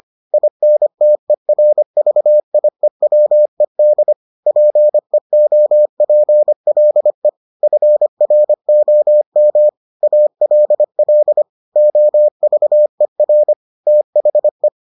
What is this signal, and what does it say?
Morse code, heard as a keyed tone, about 25 wpm, 600 Hz. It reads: INTERVIEWED PEOPLE FROM ALL OVER THE